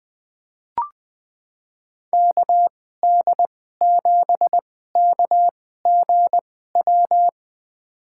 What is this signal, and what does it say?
Morse code, heard as a keyed tone, about 20 wpm, spaced as for 14 wpm, 700 Hz. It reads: KD7KGW
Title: USA Call Signs - 6 Characters - 2x Character Spacing - 20wpm